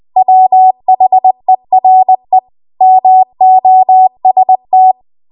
Morse code, heard as a keyed tone, about 20 wpm, 750 Hz. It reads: WHERE MOST
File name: Archive / 20 WPM / 20WPM